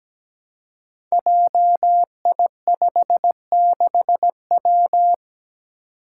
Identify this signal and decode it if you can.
Morse code, heard as a keyed tone, about 17 wpm, 700 Hz. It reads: JI56W
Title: Five Challenging Letter-Number Combinations 17wpm